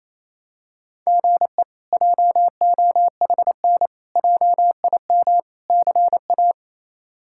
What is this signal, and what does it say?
Morse code, heard as a keyed tone, about 28 wpm, 700 Hz. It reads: GE JOHN JIM CA
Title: K1USN SST QSO Practice - 28wpm